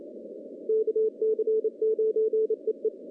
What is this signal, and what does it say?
Morse code, heard as a keyed tone, about 28 wpm, 440 Hz. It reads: KC9EE
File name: contests / CW-SS-k5tr- -1400